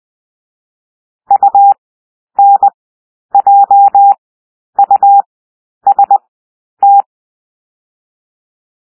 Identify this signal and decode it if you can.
Morse code, heard as a keyed tone, about 20 wpm, 800 Hz. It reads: U N J U S T
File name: cwmp3 / W1 SOUND WORDS 20 wpm